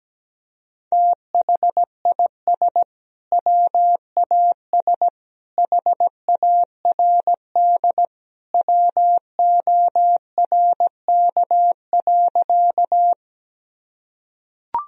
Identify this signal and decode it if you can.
Morse code, heard as a keyed tone, about 17 wpm, 700 Hz. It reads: THIS WAS HARD WORK.
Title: Sentences from Top 1000 Words - Review Part 1 - 17wpm